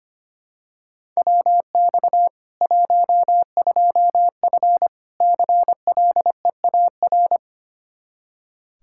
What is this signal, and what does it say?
Morse code, heard as a keyed tone, about 25 wpm, 700 Hz. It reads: WX 12F CLEAR
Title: Practice QSO - Exchange 2 - Info - 25wpm